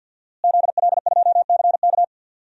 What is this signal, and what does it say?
Morse code, heard as a keyed tone, about 51 wpm, 700 Hz. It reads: ZL2CK
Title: DXCC-100 Call Signs - 5 Characters - Repeated 3x - 50wpm